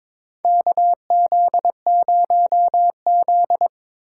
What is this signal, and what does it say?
Morse code, heard as a keyed tone, about 22 wpm, 700 Hz. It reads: KZ0Z
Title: USA Call Signs - 4 Characters - 3x Character Spacing - 22wpm